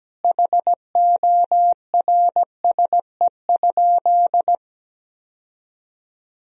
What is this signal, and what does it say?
Morse code, heard as a keyed tone, about 17 wpm, 700 Hz. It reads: HORSE?
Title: Sentences from Top 1000 Words - Review Part 1 - 17wpm